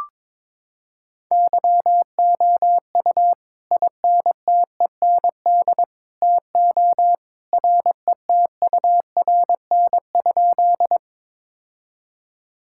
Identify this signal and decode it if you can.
Morse code, heard as a keyed tone, about 22 wpm, 700 Hz. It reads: YOU INTEND TO RETURN?
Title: Sentences from Top 1600 Words 22wpm